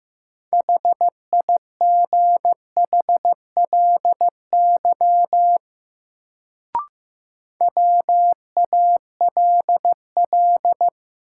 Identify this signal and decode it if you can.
Morse code, heard as a keyed tone, about 15 wpm, 700 Hz. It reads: HIGHLY WALL
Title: Sentences from Top 1100 Words - New Word Drill Alt - 15wpm